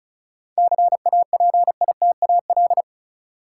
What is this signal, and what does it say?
Morse code, heard as a keyed tone, about 36 wpm, 700 Hz. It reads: CAPITAL
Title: Sentences from Top 1000 Words - New Word Drill - 35wpm